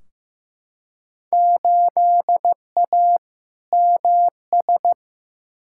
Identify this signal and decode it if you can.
Morse code, heard as a keyed tone, about 15 wpm, 700 Hz. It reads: 8A MS
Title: Field Day Exchanges - 15wpm